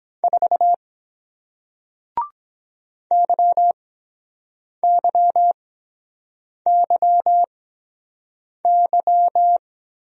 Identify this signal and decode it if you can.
Morse code, heard as a keyed tone, about 23 wpm, spaced as for 7 wpm, 700 Hz. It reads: 4 YYYY